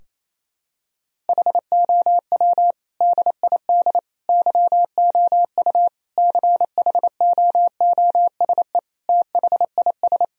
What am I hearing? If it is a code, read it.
Morse code, heard as a keyed tone, about 27 wpm, 700 Hz. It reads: HOW DID YOU CHOOSE THIS
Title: Sentences from Top 1000 Words - Review Part 4 - 28wpm